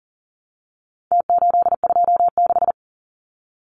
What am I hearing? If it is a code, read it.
Morse code, heard as a keyed tone, about 40 wpm, 700 Hz. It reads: T826
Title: Numbers - Contest Style - 4-Digit 40wpm